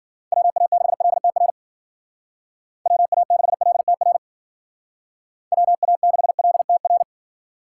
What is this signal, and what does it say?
Morse code, heard as a keyed tone, about 52 wpm, 700 Hz. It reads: WA6LTR WA6LTR WA6LTR